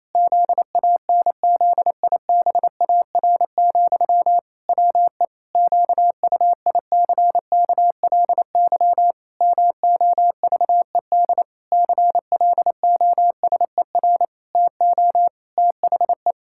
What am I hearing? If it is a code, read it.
Morse code, heard as a keyed tone, about 28 wpm, 700 Hz. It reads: ZANZIBAR, WE QUICKLY MOVED CLOSER TO THE
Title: Pangram Challenge v2 - 28wpm